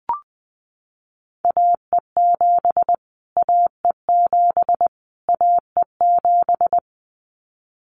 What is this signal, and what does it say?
Morse code, heard as a keyed tone, about 20 wpm, 700 Hz. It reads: AE7 AE7 AE7
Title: Callsign Prefixes for Top 25 DXCC - Repeated 3x - 20wpm